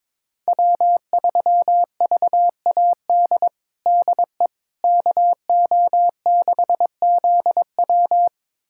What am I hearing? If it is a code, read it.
Morse code, heard as a keyed tone, about 22 wpm, 700 Hz. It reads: W3VAD DE KO6ZW
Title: USA Callsign DE Callsign - 5 Characters - 22wpm